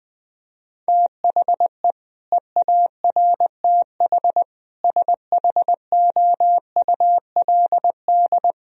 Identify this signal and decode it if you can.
Morse code, heard as a keyed tone, about 20 wpm, 700 Hz. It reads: THE EARTH SHOULD
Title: Sentences from Top 1000 Words - Review Part 3 - 20wpm